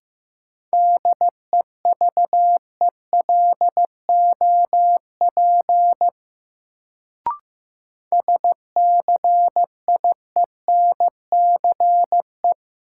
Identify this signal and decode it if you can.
Morse code, heard as a keyed tone, about 15 wpm, 700 Hz. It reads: DEVELOP SCIENCE
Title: Sentences from Top 1000 Words - Review Part 2 - New Word Drill - 15wpm